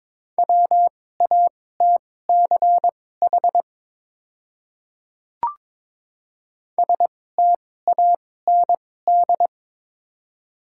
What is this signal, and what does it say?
Morse code, heard as a keyed tone, about 22 wpm, spaced as for 16 wpm, 700 Hz. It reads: WATCH STAND
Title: 2x Character Spacing - Top 300 Words - 22wpm